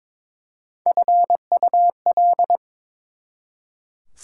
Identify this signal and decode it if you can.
Morse code, heard as a keyed tone, about 22 wpm, 700 Hz. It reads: FUL
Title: Prefixes and Suffixes 22wpm